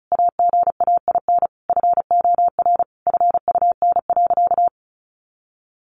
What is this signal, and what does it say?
Morse code, heard as a keyed tone, about 34 wpm, 700 Hz. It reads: AGAIN FOR FUN.
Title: Easy Phrases 35wpm